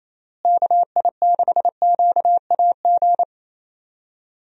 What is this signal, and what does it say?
Morse code, heard as a keyed tone, about 27 wpm, 700 Hz. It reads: KI6QAG
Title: USA Call Signs - 6 Characters - Repeated 3x - 28wpm